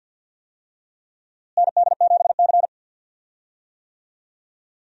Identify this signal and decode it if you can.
Morse code, heard as a keyed tone, about 52 wpm, 700 Hz. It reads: ND7C
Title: USA Call Signs - 4 Characters - Repeated 3x - 50wpm